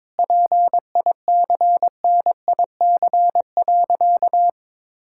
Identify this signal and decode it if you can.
Morse code, heard as a keyed tone, about 22 wpm, 700 Hz. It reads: PICNIC.
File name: Difficult Sentences with Hints - 22wpm